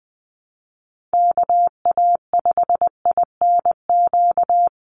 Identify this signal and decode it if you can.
Morse code, heard as a keyed tone, about 20 wpm, 700 Hz. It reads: KA5INQ
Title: USA Call Signs 20wpm